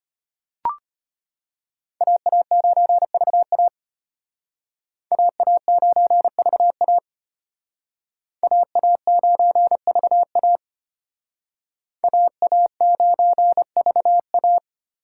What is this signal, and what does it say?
Morse code, heard as a keyed tone, about 32 wpm, 700 Hz. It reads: AA9VA AA9VA AA9VA AA9VA